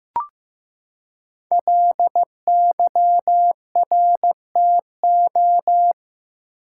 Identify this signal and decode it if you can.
Morse code, heard as a keyed tone, about 15 wpm, 700 Hz. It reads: LYRTO
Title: Code Groups - 5 Letters - 15wpm